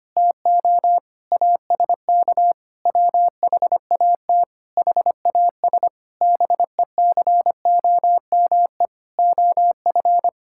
Morse code, heard as a keyed tone, about 25 wpm, 700 Hz. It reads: TO ASK WHAT HAS BECOME OF
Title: Sentences from Top 1000 Words - Review Part 5 - 25wpm